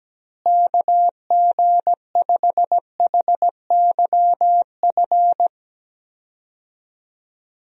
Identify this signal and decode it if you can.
Morse code, heard as a keyed tone, about 17 wpm, 700 Hz. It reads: KG5HYF
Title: USA Call Signs - 6 Characters - Repeated 3x - 17wpm